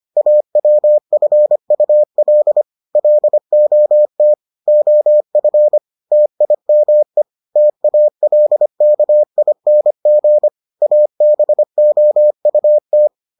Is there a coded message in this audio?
Yes: Morse code from 0.2 to 13.1 s, heard as a keyed tone, about 25 wpm, 600 Hz. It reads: AWFUL LOT OF TIME TALKING ABOUT